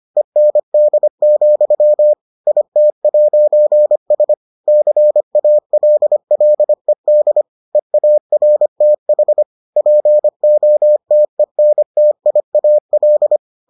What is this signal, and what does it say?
Morse code, heard as a keyed tone, about 25 wpm, 600 Hz. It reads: END, IT'S CALLED EARTH POTENTIAL